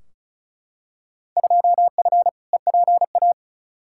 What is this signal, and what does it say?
Morse code, heard as a keyed tone, about 34 wpm, 700 Hz. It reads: 2F EPA
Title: Field Day Exchanges - 35wpm